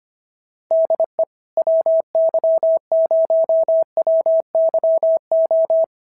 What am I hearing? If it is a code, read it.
Morse code, heard as a keyed tone, about 25 wpm, 650 Hz. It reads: DE WY0WYO
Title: SOTA QSO Practice 25wpm